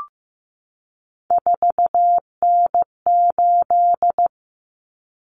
Morse code, heard as a keyed tone, about 15 wpm, 700 Hz. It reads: 4N8
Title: Common Cut Numbers - 3-Digit 15wpm